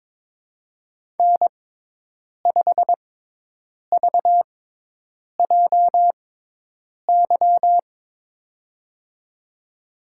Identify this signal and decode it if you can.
Morse code, heard as a keyed tone, about 21 wpm, spaced as for 8 wpm, 700 Hz. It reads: N5VJY
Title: USA Call Signs - 6x Character Spacing - 22wpm